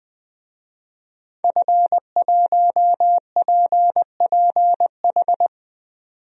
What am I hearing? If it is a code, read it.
Morse code, heard as a keyed tone, about 20 wpm, 700 Hz. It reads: F1PPH
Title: DXCC-100 Call Signs - 5 Characters - 20wpm